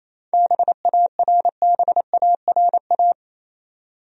Morse code, heard as a keyed tone, about 28 wpm, 700 Hz. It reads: BARBARA
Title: USA Common Names - 28wpm